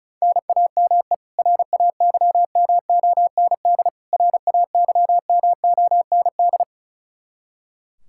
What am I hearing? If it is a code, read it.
Morse code, heard as a keyed tone, about 36 wpm, 700 Hz. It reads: NAME RAYMOND RAYMOND
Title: Practice QSO - Exchange 1 - Beginning - 35wpm